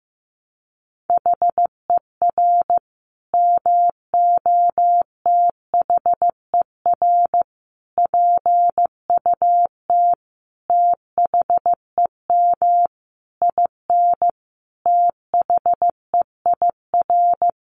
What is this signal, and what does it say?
Morse code, heard as a keyed tone, about 14 wpm, 700 Hz. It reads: HER MOTHER PUT THEM IN THEIR
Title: Sentences from Top 800 Words 15wpm